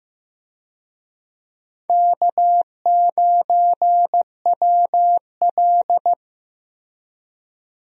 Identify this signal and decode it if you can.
Morse code, heard as a keyed tone, about 15 wpm, 700 Hz. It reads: K9WL
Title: USA Call Signs - 4 Characters - Repeated 2x - 15wpm